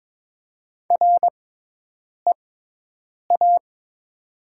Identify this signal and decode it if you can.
Morse code, heard as a keyed tone, about 22 wpm, spaced as for 8 wpm, 700 Hz. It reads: REA
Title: 6x Character Spacing - Top 300 Words - 22wpm